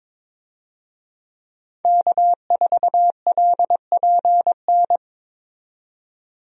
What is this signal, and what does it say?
Morse code, heard as a keyed tone, about 22 wpm, 700 Hz. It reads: K4LPN
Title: USA Call Signs - 5 Characters - 22wpm